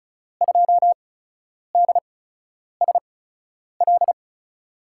Morse code, heard as a keyed tone, about 36 wpm, spaced as for 10 wpm, 700 Hz. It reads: 2DSL
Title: USA Call Signs - 5 Characters - 8x Character Spacing - 35wpm